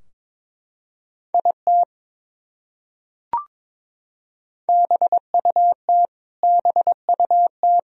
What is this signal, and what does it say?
Morse code, heard as a keyed tone, about 21 wpm, 700 Hz. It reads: IT BUT BUT
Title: Top 100 Words Repeated 22wpm